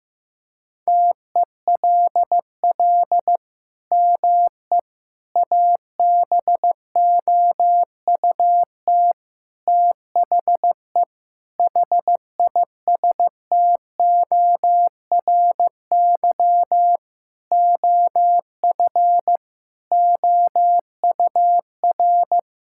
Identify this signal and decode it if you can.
Morse code, heard as a keyed tone, about 15 wpm, 700 Hz. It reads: TELL ME ABOUT THE HISTORY OF OUR
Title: Sentences from Top 1000 Words - Review Part 6 - 15wpm